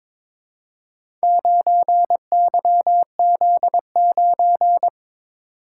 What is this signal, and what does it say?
Morse code, heard as a keyed tone, about 22 wpm, 690 Hz. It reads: 9YZ9
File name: practice / 22wpm / Four Letter-Number Combinations - 22wpm